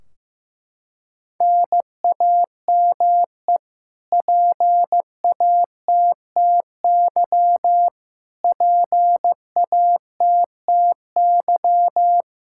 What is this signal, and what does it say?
Morse code, heard as a keyed tone, about 15 wpm, 700 Hz. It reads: NAME PATTY PATTY
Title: Practice QSO - Exchange 1 - Beginning - 15wpm